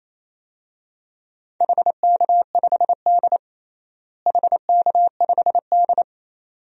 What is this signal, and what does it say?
Morse code, heard as a keyed tone, about 27 wpm, 700 Hz. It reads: HK5D HK5D
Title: DXCC-100 Call Signs - 4 Characters - Repeated 2x - 28wpm